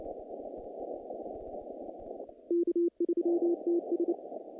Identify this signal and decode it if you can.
Morse code, heard as a keyed tone, about 28 wpm, 340 Hz. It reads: K3TS
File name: contests / naqp-cw-k5tr-op-k5pi- -2300